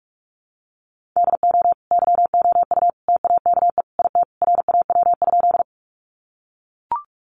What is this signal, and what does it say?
Morse code, heard as a keyed tone, about 44 wpm, 700 Hz. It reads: DO YOU TAKE IT RAW?
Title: Sentences from Top 1500 Words 45wpm